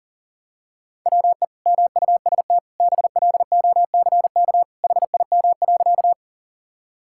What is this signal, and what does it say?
Morse code, heard as a keyed tone, about 40 wpm, 700 Hz. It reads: WE MUST BLOCK HIM.
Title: Sentences from Top 1600 Words 40wpm